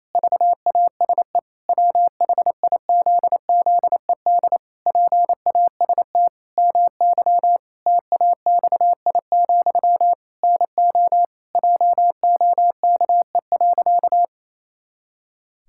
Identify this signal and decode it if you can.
Morse code, heard as a keyed tone, about 28 wpm, 700 Hz. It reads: VASE WHIZZED PAST MY TAXI, NO JOKE.
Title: Pangram Challenge v2 - 28wpm